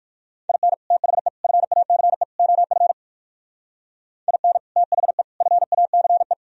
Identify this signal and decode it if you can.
Morse code, heard as a keyed tone, about 59 wpm, 700 Hz. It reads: IN THE FACE OF IN THE FACE